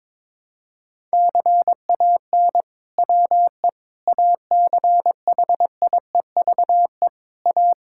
Morse code, heard as a keyed tone, about 22 wpm, 700 Hz. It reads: CAN WE ACHIEVE A
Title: Sentences from Top 1000 Words - Review Part 6 - 22wpm